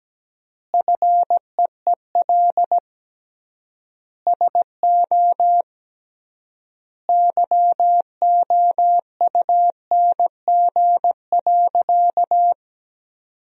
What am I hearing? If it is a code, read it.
Morse code, heard as a keyed tone, about 17 wpm, 700 Hz. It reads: FEEL SO YOUNG.
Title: 3x Word-Spacing - Sentences from Top 400 Words - 17wpm